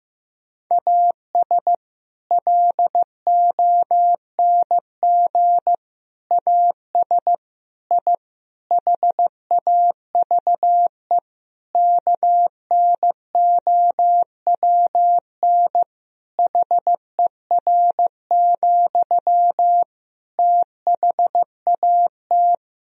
Morse code, heard as a keyed tone, about 15 wpm, 700 Hz. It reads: AS LONG AS I HAVE KNOWN HER, THAT
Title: Sentences from Top 1000 Words - Review Part 3 - 15wpm